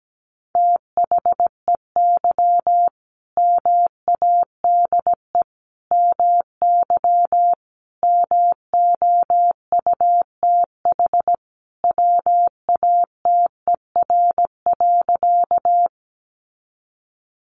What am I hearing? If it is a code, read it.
Morse code, heard as a keyed tone, about 17 wpm, 700 Hz. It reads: THEY MADE MY MOUTH WATER.
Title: Sentences from Top 1500 Words 17wpm